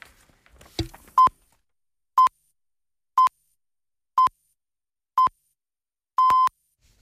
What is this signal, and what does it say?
Radio time signal (time pips). It is heard from 1.2 to 6.5 s.